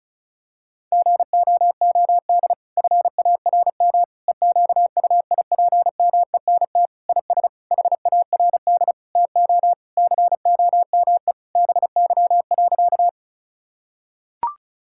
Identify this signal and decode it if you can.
Morse code, heard as a keyed tone, about 36 wpm, 700 Hz. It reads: GOOD FARM EQUIPMENT IS HARD TO COME BY.
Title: Sentences from Top 1000 Words - Review Part 5 - 35wpm